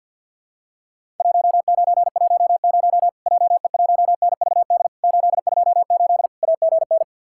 Morse code, heard as a keyed tone, about 50 wpm, 700 Hz. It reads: 1910 JE1NVD 828 AGN
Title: Worked All Europe DX Contest - QTC Exchange with Repeats - 50wpm